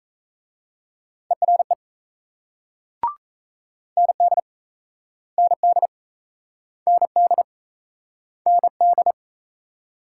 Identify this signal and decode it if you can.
Morse code, heard as a keyed tone, about 36 wpm, 700 Hz. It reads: ERE ND ND ND ND